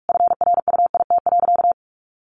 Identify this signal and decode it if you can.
Morse code, heard as a keyed tone, about 46 wpm, 700 Hz. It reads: FRUIT.
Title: Sentences from Top 1300 Words 45wpm